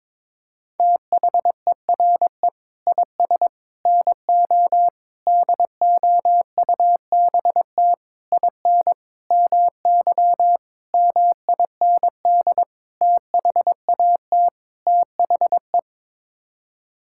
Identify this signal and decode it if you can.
Morse code, heard as a keyed tone, about 22 wpm, 700 Hz. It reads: THERE IS NO DOUBT IN MY MIND THAT THE
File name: Sets of 9 Words - 22wpm